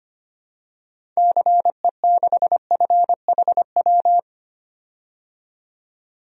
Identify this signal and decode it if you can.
Morse code, heard as a keyed tone, about 25 wpm, 700 Hz. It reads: CE6FHW
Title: DXCC-100 Call Signs - 6 Characters - 25wpm